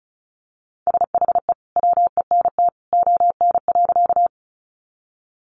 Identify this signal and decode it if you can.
Morse code, heard as a keyed tone, about 34 wpm, 700 Hz. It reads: SHE WENT ON.
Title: Sentences from Top 200 Words 35wpm